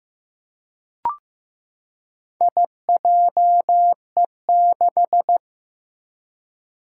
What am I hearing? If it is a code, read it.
Morse code, heard as a keyed tone, about 16 wpm, 700 Hz. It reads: IJE6
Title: Four Challenging Letter-Number Combinations 15wpm